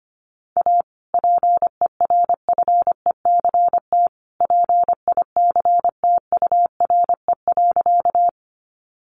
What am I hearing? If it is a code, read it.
Morse code, heard as a keyed tone, about 25 wpm, 700 Hz. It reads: A PERFECT PICTURE.
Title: Sentences from Top 1000 Words 25wpm